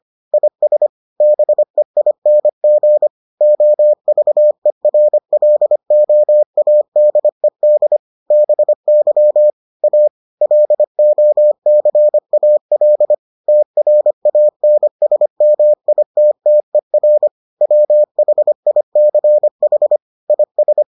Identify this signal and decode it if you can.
Morse code, heard as a keyed tone, about 25 wpm, 600 Hz. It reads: IS BEING OVERLOADED BY A LOCAL TRANSMITTER WHICH IS